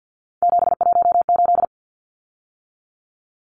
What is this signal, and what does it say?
Morse code, heard as a keyed tone, about 50 wpm, 700 Hz. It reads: 718
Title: Numbers - 3-Digit 50wpm